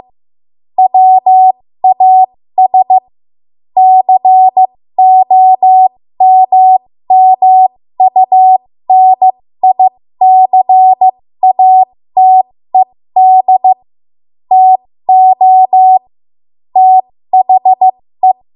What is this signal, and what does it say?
Morse code, heard as a keyed tone, about 15 wpm, spaced as for 13 wpm, 750 Hz. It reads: WAS COMMUNICATED TO THE
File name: Morse / 13WPM